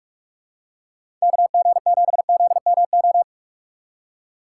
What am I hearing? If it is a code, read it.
Morse code, heard as a keyed tone, about 44 wpm, 700 Hz. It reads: KG7ZMO
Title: USA Call Signs - 6 Characters - 3x Character Spacing - 45wpm